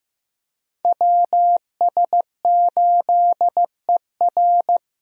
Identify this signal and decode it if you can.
Morse code, heard as a keyed tone, about 15 wpm, 700 Hz. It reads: WS8ER
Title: USA Call Signs - 5 Characters - Repeated 3x - 15wpm